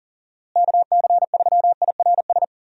Morse code, heard as a keyed tone, about 37 wpm, 700 Hz. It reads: KC3IRS
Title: USA Call Signs - 6 Characters - Repeated 3x - 40wpm